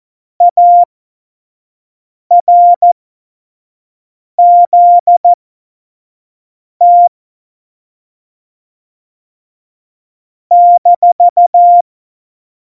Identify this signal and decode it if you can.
Morse code, heard as a keyed tone, about 14 wpm, spaced as for 5 wpm, 700 Hz. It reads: ARZT -
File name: Morsekurs / Witze / Witz 06. Februar 2018, 25 F70